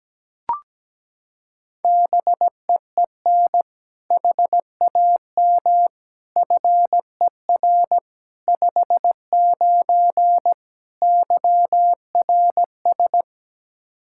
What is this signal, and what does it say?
Morse code, heard as a keyed tone, about 17 wpm, 700 Hz. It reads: BEEN HAM FER 59 YRS
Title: Practice QSO - Exchange 3 - More Info - 17wpm